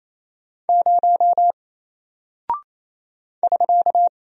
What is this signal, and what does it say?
Morse code, heard as a keyed tone, about 27 wpm, 700 Hz. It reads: 0 <SK>
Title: Single Character (Letter:Number:Punctuation:Prosign) 28wpm